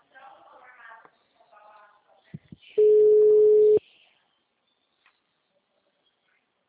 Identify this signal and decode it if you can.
Ringback tone (Europe, 425 Hz), with speech.
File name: out- -2020- - - .43677